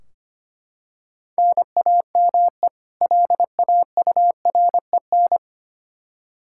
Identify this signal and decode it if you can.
Morse code, heard as a keyed tone, about 25 wpm, 700 Hz. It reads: NAME LAUREN
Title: K1USN SST QSO Practice 25wpm